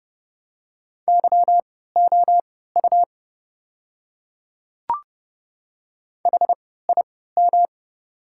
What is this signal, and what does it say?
Morse code, heard as a keyed tone, about 30 wpm, spaced as for 17 wpm, 700 Hz. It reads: YOU HIM
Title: 3x Character Spacing - Top 100 Words - 30wpm